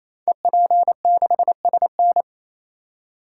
Morse code, heard as a keyed tone, about 27 wpm, 700 Hz. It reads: EP6SN
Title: Code Groups - 5 Letter:Numbers - 28wpm